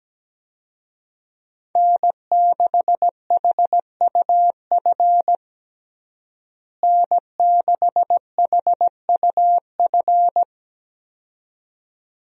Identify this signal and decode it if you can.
Morse code, heard as a keyed tone, about 17 wpm, 700 Hz. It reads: N6HUF N6HUF
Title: USA Call Signs - 5 Characters - Repeated 2x - 17wpm